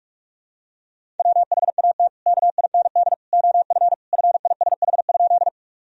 Morse code, heard as a keyed tone, about 42 wpm, 700 Hz. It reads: WHAT KIND OF FISH?
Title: Sentences from Top 1000 Words - Review Part 1 - 45wpm